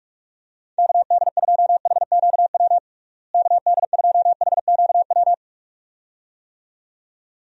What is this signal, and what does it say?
Morse code, heard as a keyed tone, about 46 wpm, 700 Hz. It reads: KD2HQW KD2HQW
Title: USA Call Signs - 6 Characters - Repeated 2x - 45wpm